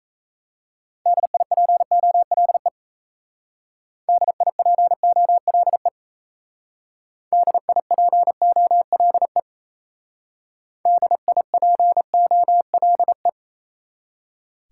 Morse code, heard as a keyed tone, about 36 wpm, 700 Hz. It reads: DIPOLE DIPOLE DIPOLE DIPOLE